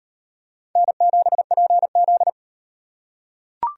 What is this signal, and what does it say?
Morse code, heard as a keyed tone, about 38 wpm, 700 Hz. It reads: N7PZ